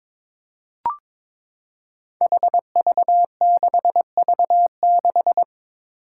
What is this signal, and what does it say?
Morse code, heard as a keyed tone, about 22 wpm, 700 Hz. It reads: HV6V6
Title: Five Challenging Letter-Number Combinations 22wpm